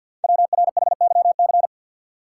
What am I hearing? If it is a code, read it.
Morse code, heard as a keyed tone, about 50 wpm, 690 Hz. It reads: WRHYC
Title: Code Groups - 5 Letters - 50wpm